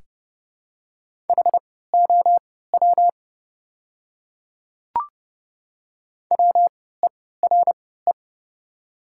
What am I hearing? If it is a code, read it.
Morse code, heard as a keyed tone, about 30 wpm, spaced as for 17 wpm, 700 Hz. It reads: HOW WERE
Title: 3x Character Spacing - Top 100 Words - 30wpm